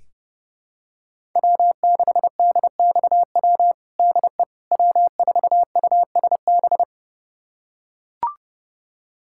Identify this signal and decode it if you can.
Morse code, heard as a keyed tone, about 30 wpm, 700 Hz. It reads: W6DXW DE W4USB
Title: USA Callsign DE Callsign - 5 Characters - 30wpm